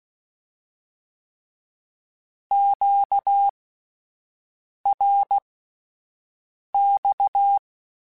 Morse code, heard as a keyed tone, about 16 wpm, spaced as for 5 wpm, 800 Hz. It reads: QRX
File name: (spaced) / Q-Calls Spaced-mp3s / Q Calls sp Once @10